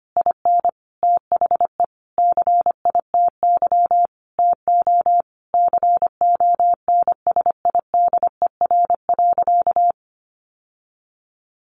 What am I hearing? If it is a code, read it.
Morse code, heard as a keyed tone, about 25 wpm, 700 Hz. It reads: IN THE CITY TO CONSIDER.
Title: Sentences from Top 900 Words 25wpm